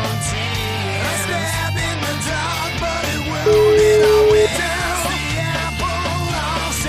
Ringback tone (Europe, 425 Hz), with noise.